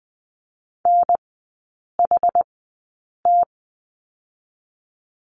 Morse code, heard as a keyed tone, about 20 wpm, spaced as for 8 wpm, 700 Hz. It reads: NHT
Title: 060 Lesson 11e - Letter h - Three Characters 20wpm